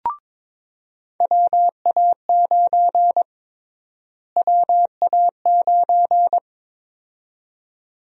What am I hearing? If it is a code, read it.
Morse code, heard as a keyed tone, about 22 wpm, 700 Hz. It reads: WA9 WA9
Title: USA Call Sign Prefixes - Repeated 2x - 22wpm